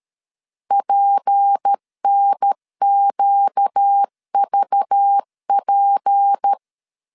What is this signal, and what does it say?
Morse code, heard as a keyed tone, about 13 wpm, 800 Hz. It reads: PNQVP
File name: LEZIONI / 6 PKX0 / 6 P 12wpm 2033